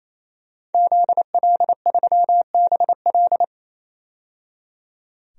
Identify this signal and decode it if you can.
Morse code, heard as a keyed tone, about 28 wpm, 700 Hz. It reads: ZL3BL
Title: DXCC-100 Call Signs - 5 Characters - 28wpm